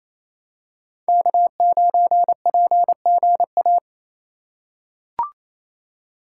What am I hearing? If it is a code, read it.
Morse code, heard as a keyed tone, about 28 wpm, 700 Hz. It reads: K9PGA
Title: USA Call Signs - 3x Character Spacing - 28wpm